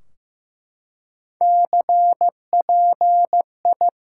Morse code, heard as a keyed tone, about 15 wpm, 700 Hz. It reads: CPI
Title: Top 180 QSO Elements 15wpm